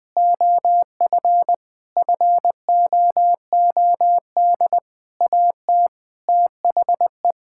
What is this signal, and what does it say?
Morse code, heard as a keyed tone, about 20 wpm, 700 Hz. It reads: OF FOOD AT THE
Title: Sentences with 5-Word Phrases - Spoken Beginning - 20wpm